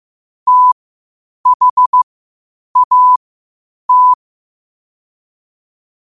Morse code, heard as a keyed tone, about 15 wpm, spaced as for 8 wpm, 1000 Hz. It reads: THAT
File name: qotdfiles / 8wpm